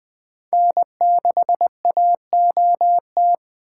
Morse code, heard as a keyed tone, about 20 wpm, 700 Hz. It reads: N6AOT
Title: USA Call Signs - 5 Characters - 4x Character Spacing - 20wpm